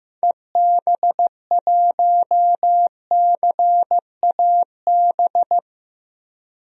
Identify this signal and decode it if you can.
Morse code, heard as a keyed tone, about 15 wpm, 700 Hz. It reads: EB1CAB
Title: DXCC-100 Call Signs - 6 Characters - 15wpm